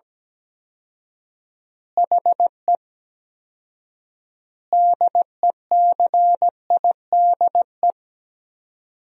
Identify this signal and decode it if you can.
Morse code, heard as a keyed tone, about 17 wpm, 700 Hz. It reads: HE DECIDE
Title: 4x Word-Spacing - Sentences from Top 500 Words - 17wpm